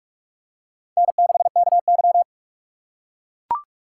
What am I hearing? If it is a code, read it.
Morse code, heard as a keyed tone, about 45 wpm, 700 Hz. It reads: N6KY